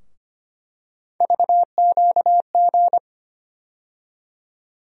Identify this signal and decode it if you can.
Morse code, heard as a keyed tone, about 25 wpm, 700 Hz. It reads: VQG
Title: USA Call Sign Postfixes - 3 Letters 25wpm